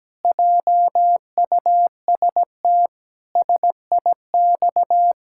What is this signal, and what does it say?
Morse code, heard as a keyed tone, about 17 wpm, 700 Hz. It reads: JUST SIX
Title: Pangram Challenge v2 - 17wpm